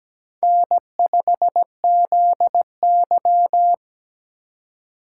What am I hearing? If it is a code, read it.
Morse code, heard as a keyed tone, about 17 wpm, 700 Hz. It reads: N5ZY
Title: USA Call Signs - 4 Characters - 4x Character Spacing - 17wpm